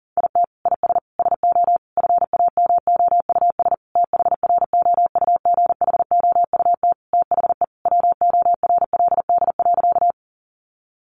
Morse code, heard as a keyed tone, about 37 wpm, 700 Hz. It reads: IT IS SO FAMOUS THROUGHOUT THE WORLD.
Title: Sentences from Top 1500 Words 40wpm